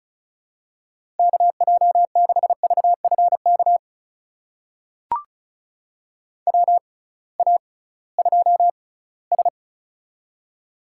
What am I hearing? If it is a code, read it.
Morse code, heard as a keyed tone, about 36 wpm, spaced as for 20 wpm, 700 Hz. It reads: KJ6VFK WA2S
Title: USA Call Signs - 6x Character Spacing - 35wpm